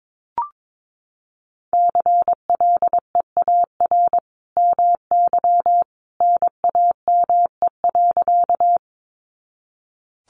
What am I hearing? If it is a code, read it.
Morse code, heard as a keyed tone, about 22 wpm, 700 Hz. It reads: CLEAR MY NAME.
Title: Sentences from Top 500 Words 22wpm